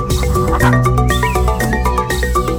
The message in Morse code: none